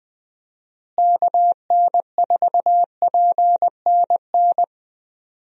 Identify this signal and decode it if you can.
Morse code, heard as a keyed tone, about 20 wpm, 700 Hz. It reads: KN4PNN
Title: USA Call Signs - 6 Characters - Repeated 3x - 20wpm